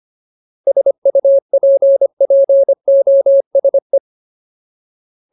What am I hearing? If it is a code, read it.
Morse code, heard as a keyed tone, about 25 wpm, 550 Hz. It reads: SUPPOSE
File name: Ultimate Word List / 25WPM Ultimate